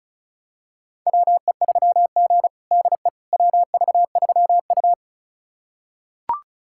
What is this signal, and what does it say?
Morse code, heard as a keyed tone, about 36 wpm, 700 Hz. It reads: WE3G DE WV3U
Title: USA Callsign DE Callsign - 4 Characters - 35wpm